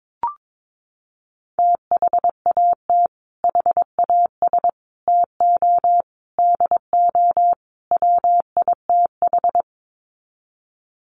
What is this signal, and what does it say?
Morse code, heard as a keyed tone, about 22 wpm, 700 Hz. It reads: THAT HAS TO DO WITH
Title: Sets of 5 Words Encore2 22wpm